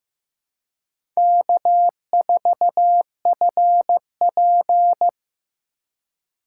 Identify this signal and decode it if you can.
Morse code, heard as a keyed tone, about 18 wpm, 700 Hz. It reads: K4FP
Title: Speed Racing - USA 1x2 Call Signs 15wpm